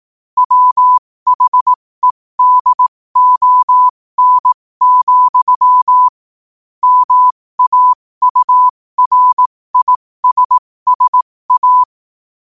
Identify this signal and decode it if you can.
Morse code, heard as a keyed tone, about 18 wpm, spaced as for 15 wpm, 1000 Hz. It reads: WHEDON, MAURISSA
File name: qotdfiles / 15wpm